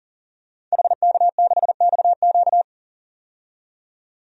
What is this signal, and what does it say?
Morse code, heard as a keyed tone, about 40 wpm, 700 Hz. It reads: HK6XQ
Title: DXCC-100 Call Signs - 5 Characters - 40wpm